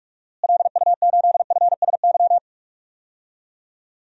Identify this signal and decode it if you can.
Morse code, heard as a keyed tone, about 45 wpm, 700 Hz. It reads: LU8FSY
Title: DXCC-100 Call Signs - 6 Characters - Repeated 2x - 45wpm